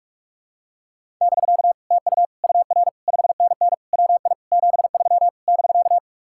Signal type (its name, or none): Morse code